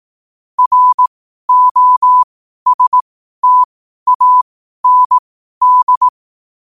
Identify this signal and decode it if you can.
Morse code, heard as a keyed tone, about 18 wpm, spaced as for 12 wpm, 1000 Hz. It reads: ROSTAND
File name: qotdfiles / 12wpm